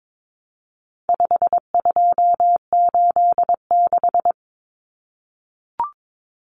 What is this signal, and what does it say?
Morse code, heard as a keyed tone, about 22 wpm, 700 Hz. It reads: 5286
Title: Numbers - 4-Digit 22wpm